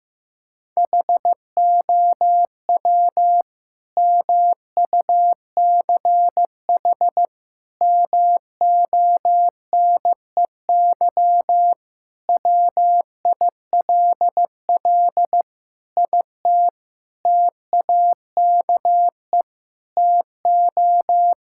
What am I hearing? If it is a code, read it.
Morse code, heard as a keyed tone, about 15 wpm, 700 Hz. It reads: HOW MUCH MONEY WILL IT TAKE TO
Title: Sentences from Top 1000 Words - Review Part 5 - 15wpm